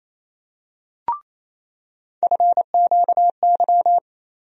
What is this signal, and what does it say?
Morse code, heard as a keyed tone, about 28 wpm, 700 Hz. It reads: FQY
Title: USA Call Sign Postfixes - 3 Letters 28wpm